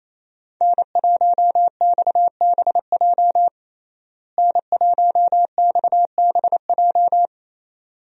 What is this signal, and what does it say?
Morse code, heard as a keyed tone, about 27 wpm, 700 Hz. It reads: N1XBJ N1XBJ
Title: USA Call Signs - 5 Characters - Repeated 2x - 28wpm